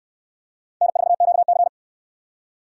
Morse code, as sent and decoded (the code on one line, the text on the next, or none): -. ....- --.- -.-.
N4QC